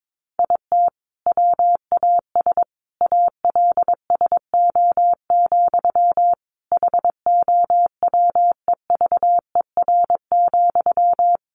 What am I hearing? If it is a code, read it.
Morse code, heard as a keyed tone, about 22 wpm, 700 Hz. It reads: IT WAS ALSO, HOWEVER,
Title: Sentences from Top 1400 Words 22wpm